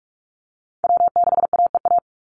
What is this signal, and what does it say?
Morse code, heard as a keyed tone, about 44 wpm, 700 Hz. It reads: W6AEA
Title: Call Signs - 45wpm